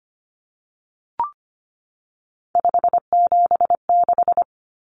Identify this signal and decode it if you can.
Morse code, heard as a keyed tone, about 25 wpm, 700 Hz. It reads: 576
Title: Common Cut Numbers - 3-Digit 25wpm